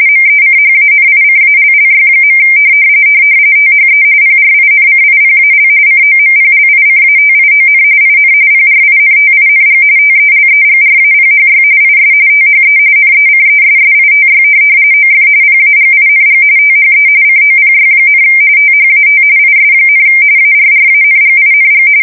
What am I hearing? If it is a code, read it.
RTTY, about 45.45 baud. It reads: MECHANICALLY. THE PRINCIPAL COMPLICATION IN PRINTING TELEGRAPHY IS THE NECESSITY OF PAGING THE COPY, THAT IS, ARRANGING
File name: fsk170 / printing-telegraph